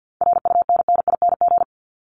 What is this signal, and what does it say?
Morse code, heard as a keyed tone, about 50 wpm, 700 Hz. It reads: RUNNING
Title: ING Suffix - Top 100 Words with Base Words 50wpm